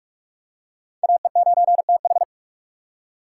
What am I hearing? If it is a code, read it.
Morse code, heard as a keyed tone, about 47 wpm, 700 Hz. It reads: AE9TH
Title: USA Call Signs - 5 Characters - 2x Character Spacing - 45wpm